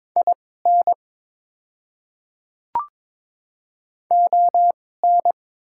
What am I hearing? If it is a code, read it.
Morse code, heard as a keyed tone, about 23 wpm, spaced as for 16 wpm, 690 Hz. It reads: IN ON
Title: 2x Character Spacing - Top 100 Words - 22wpm